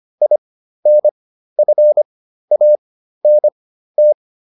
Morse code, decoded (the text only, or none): INFANT